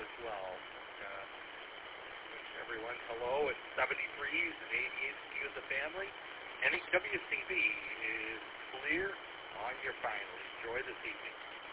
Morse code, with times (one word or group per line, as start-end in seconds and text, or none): none